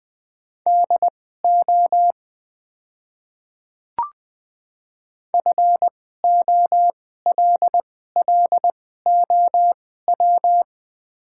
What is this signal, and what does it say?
Morse code, heard as a keyed tone, about 20 wpm, spaced as for 14 wpm, 700 Hz. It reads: DO FOLLOW
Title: 2x Character Spacing - Top 300 Words - 20wpm